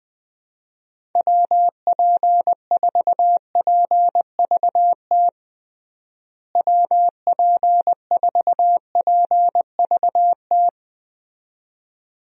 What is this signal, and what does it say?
Morse code, heard as a keyed tone, about 20 wpm, 700 Hz. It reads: WP4PVT WP4PVT
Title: USA Call Signs - 6 Characters - Repeated 2x - 20wpm